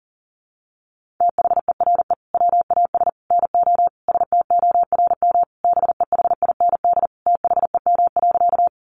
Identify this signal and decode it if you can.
Morse code, heard as a keyed tone, about 40 wpm, 700 Hz. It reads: THERE WAS NO STORM BEHIND THEM.
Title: Sentences from Top 1500 Words 40wpm